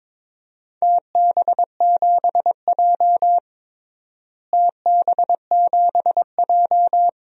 Morse code, heard as a keyed tone, about 22 wpm, 700 Hz. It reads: TB7J TB7J
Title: DXCC-100 Call Signs - 4 Characters - Repeated 3x - 22wpm